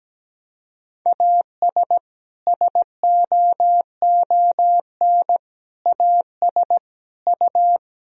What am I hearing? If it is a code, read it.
Morse code, heard as a keyed tone, about 17 wpm, 700 Hz. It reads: AS SOON AS U
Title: Sentences from Top 400 Words and Abbreviations 17wpm